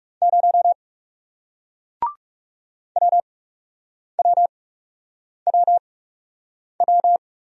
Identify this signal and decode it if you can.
Morse code, heard as a keyed tone, about 38 wpm, spaced as for 8 wpm, 700 Hz. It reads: 0 WWWW
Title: Speed-Racing Single Letter-Number 30wpm